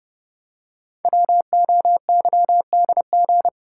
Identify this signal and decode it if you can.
Morse code, heard as a keyed tone, about 28 wpm, 700 Hz. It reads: WOYDG
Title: Code Groups - 5 Letters - 30wpm